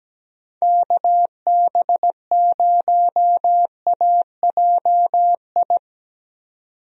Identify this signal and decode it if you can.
Morse code, heard as a keyed tone, about 17 wpm, 700 Hz. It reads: KB0AJI
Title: USA Call Signs - 6 Characters - 17wpm